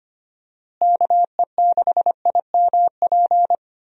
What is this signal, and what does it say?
Morse code, heard as a keyed tone, about 25 wpm, 700 Hz. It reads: KE6IMP
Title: USA Call Signs - 6 Characters - Repeated 3x - 25wpm